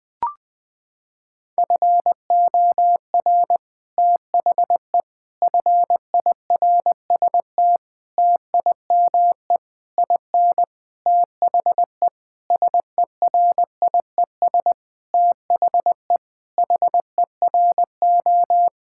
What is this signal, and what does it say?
Morse code, heard as a keyed tone, about 20 wpm, 700 Hz. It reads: FOR THE FIRST TIME IN THE SERIES THE HERO
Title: Sentences with 6-Word Phrases - 20wpm